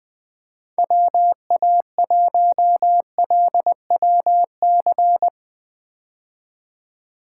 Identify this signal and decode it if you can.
Morse code, heard as a keyed tone, about 20 wpm, 700 Hz. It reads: WA1LWC
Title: K1USN SST QSO Practice - 20wpm